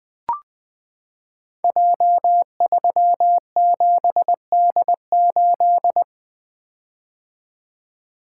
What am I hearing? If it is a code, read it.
Morse code, heard as a keyed tone, about 20 wpm, 700 Hz. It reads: J37D8
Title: Code Groups - 5 Letter:Numbers - 20wpm